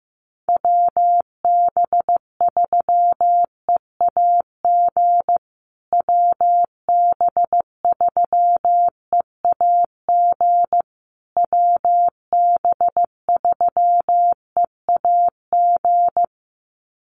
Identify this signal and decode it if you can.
Morse code, heard as a keyed tone, about 15 wpm, 700 Hz. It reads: WB3EAG WB3EAG WB3EAG
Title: USA Call Signs Repeated 3x 15wpm